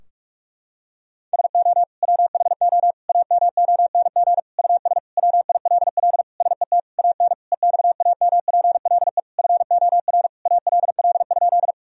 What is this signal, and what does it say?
Morse code, heard as a keyed tone, about 42 wpm, 700 Hz. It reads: SO WHO AMONG US WILL SET AN EXAMPLE FOR ALL?
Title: Sentences from Top 1000 Words - Review Part 3 - 45wpm